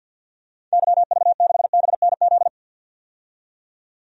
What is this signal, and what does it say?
Morse code, heard as a keyed tone, about 51 wpm, 700 Hz. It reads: YV6BNZ
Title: DXCC-100 Call Signs - 6 Characters - 50wpm